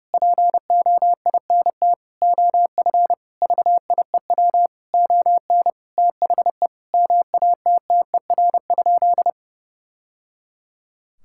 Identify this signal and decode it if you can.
Morse code, heard as a keyed tone, about 30 wpm, 700 Hz. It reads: POINT OF VIEW ON THE MATTER?
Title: Sentences from Top 1000 Words - Review Part 6 - 30wpm